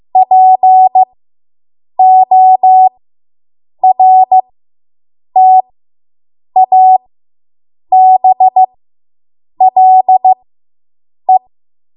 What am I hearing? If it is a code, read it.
Morse code, heard as a keyed tone, about 15 wpm, spaced as for 7 wpm, 750 Hz. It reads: PORTABLE